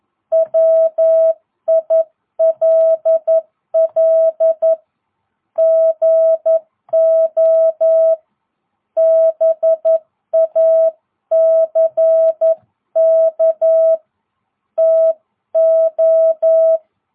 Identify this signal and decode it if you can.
Morse code, heard as a keyed tone, about 11 wpm, 650 Hz. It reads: WILL GO BACK TO